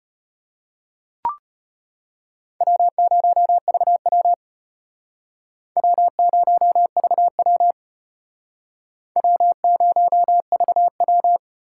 Morse code, heard as a keyed tone, about 32 wpm, 700 Hz. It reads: W0VW W0VW W0VW